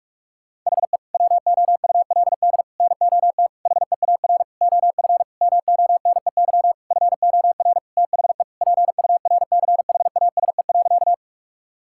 Morse code, heard as a keyed tone, about 42 wpm, 700 Hz. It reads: HE WOULD NOT HEAR OF MONEY FOR THE PURCHASE.
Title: Sentences from Top 1000 Words - Review Part 5 - 45wpm